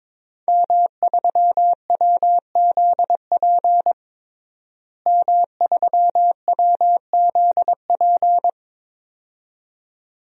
Morse code, heard as a keyed tone, about 22 wpm, 700 Hz. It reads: M3WZP M3WZP
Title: DXCC-100 Call Signs - 5 Characters - Repeated 2x - 22wpm